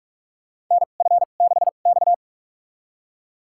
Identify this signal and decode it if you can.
Morse code, heard as a keyed tone, about 43 wpm, spaced as for 32 wpm, 690 Hz. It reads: NF6X
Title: USA Call Signs - 4 Characters - 2x Character Spacing - 45wpm